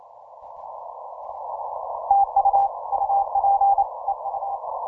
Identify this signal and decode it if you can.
Morse code, heard as a keyed tone, about 28 wpm, 800 Hz. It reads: TU UP EU
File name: simple-file-list / TX0M OC297 40M CW